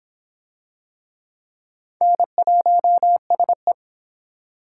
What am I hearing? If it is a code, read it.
Morse code, heard as a keyed tone, about 24 wpm, 700 Hz. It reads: N1SE